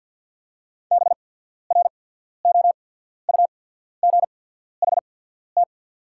Morse code, heard as a keyed tone, about 48 wpm, spaced as for 14 wpm, 690 Hz. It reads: BROUGHT
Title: 8x Character Spacing - Top 500 Words - 50wpm